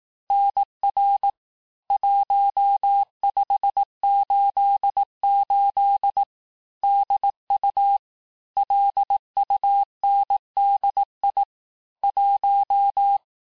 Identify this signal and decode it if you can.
Morse code, heard as a keyed tone, about 18 wpm, 800 Hz. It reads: NR 1588 DU LUNDI 1
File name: cours-du-lundi files / cours-1588-texte-chiffre-vitesse-17MPM